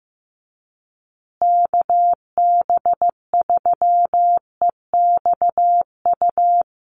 Morse code, heard as a keyed tone, about 15 wpm, 700 Hz. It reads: KB3EXU
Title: USA Call Signs Repeated 2x 15wpm